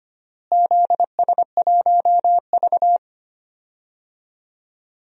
Morse code, heard as a keyed tone, about 25 wpm, 700 Hz. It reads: ZS1V
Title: DXCC-100 Call Signs - 4 Characters - Repeated 3x - 25wpm